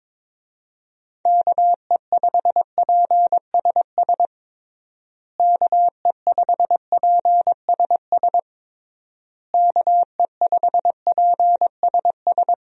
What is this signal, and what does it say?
Morse code, heard as a keyed tone, about 22 wpm, 700 Hz. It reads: KE5PSS KE5PSS KE5PSS
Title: USA Call Signs - 6 Characters - Repeated 3x - 22wpm